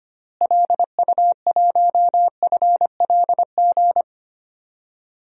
Morse code, heard as a keyed tone, about 25 wpm, 700 Hz. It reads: LU1FLG
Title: DXCC-100 Call Signs - 6 Characters - Repeated 3x - 25wpm